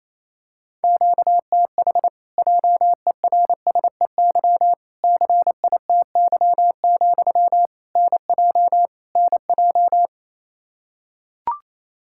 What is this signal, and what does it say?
Morse code, heard as a keyed tone, about 28 wpm, 700 Hz. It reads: QTH JERSEY CITY, NJ NJ
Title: Practice QSO - Exchange 1 - Beginning - 28wpm